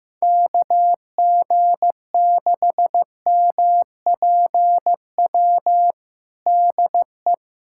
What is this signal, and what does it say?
Morse code, heard as a keyed tone, about 15 wpm, 700 Hz. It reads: KG6MPW DE
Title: USA Callsign DE Callsign - 6 Characters - 15wpm